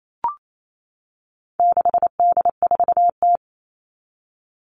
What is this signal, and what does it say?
Morse code, heard as a keyed tone, about 28 wpm, 700 Hz. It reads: 6D4T
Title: All Cut Numbers - 4-Digit 28wpm